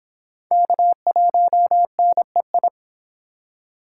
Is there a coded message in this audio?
Yes: Morse code from 0.5 to 2.7 s, heard as a keyed tone, about 30 wpm, 700 Hz. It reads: K1NEI